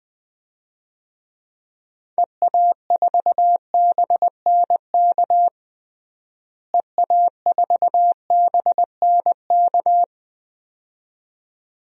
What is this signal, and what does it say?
Morse code, heard as a keyed tone, about 21 wpm, 700 Hz. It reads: EA4BNK EA4BNK
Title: DXCC-100 Call Signs - 6 Characters - Repeated 2x - 20wpm